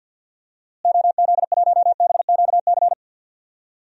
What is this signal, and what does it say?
Morse code, heard as a keyed tone, about 52 wpm, 700 Hz. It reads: OZ1BQC
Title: DXCC-100 Call Signs - 6 Characters - Repeated 2x - 50wpm